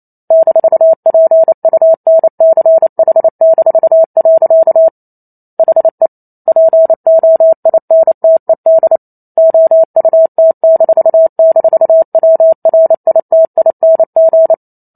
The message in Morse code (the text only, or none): -PUNCH-. HE POINTED OUT--WRITING